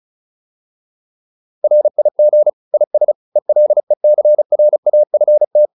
Morse code, heard as a keyed tone, about 35 wpm, 600 Hz. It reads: RIG IS ELECRAFT
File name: QSO207 35